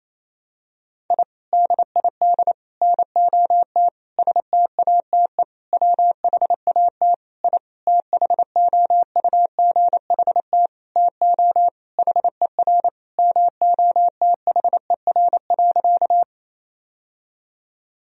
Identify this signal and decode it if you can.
Morse code, heard as a keyed tone, about 28 wpm, 700 Hz. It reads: I DID NOT STATE WHAT I THOUGHT TO HER MOTHER.
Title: Sentences from Top 1000 Words - Review Part 2 - 28wpm